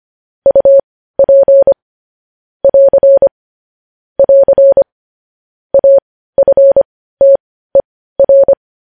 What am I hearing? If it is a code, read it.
Morse code, heard as a keyed tone, about 25 wpm, spaced as for 15 wpm, 550 Hz. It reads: UP <AR> <AR> AFTER
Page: news headlines in morse code at 15 wpm